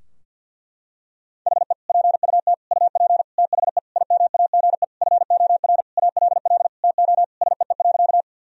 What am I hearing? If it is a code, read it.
Morse code, heard as a keyed tone, about 51 wpm, 700 Hz. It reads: HE PUT UP THE IMAGE FOR ALL TO SEE.
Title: Sentences from Top 1000 Words - Review Part 4 - 50wpm